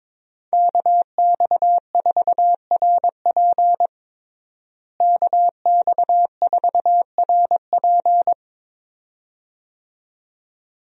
Morse code, heard as a keyed tone, about 22 wpm, 700 Hz. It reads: KX4RP KX4RP
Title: USA Call Signs - 5 Characters - Repeated 2x - 22wpm